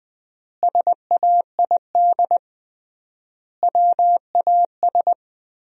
Morse code, heard as a keyed tone, about 20 wpm, 700 Hz. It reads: SAID WAS
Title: 3x Word-Spacing - Sentences from Top 400 Words - 20wpm